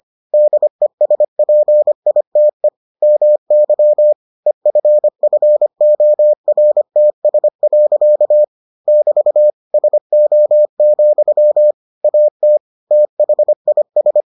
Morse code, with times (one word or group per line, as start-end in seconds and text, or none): 0.3-2.7 DESPITE
3.0-4.1 MY
4.5-8.4 EFFORTS.
8.9-9.5 <BT>
9.7-11.7 SO,
12.0-12.6 AT
12.9-14.2 THIS